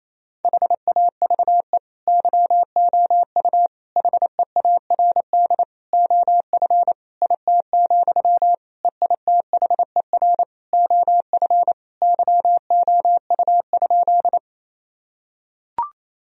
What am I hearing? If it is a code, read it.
Morse code, heard as a keyed tone, about 28 wpm, 700 Hz. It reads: HAVE YOU HEARD OF IT, EITHER OF YOU?
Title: Sentences from Top 1000 Words - Review Part 4 - 28wpm